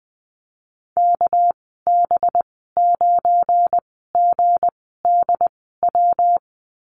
Morse code, heard as a keyed tone, about 20 wpm, spaced as for 15 wpm, 700 Hz. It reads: KB9GDW
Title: 207 Lesson 33g - Letter j - Call Signs 20wpm